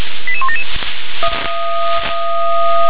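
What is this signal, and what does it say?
Five-tone selective call (ZVEI), heard as quick runs of five tones, tones of about 70 ms. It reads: 88177
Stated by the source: Received on an SDR; ZVEI five-tone code 88127.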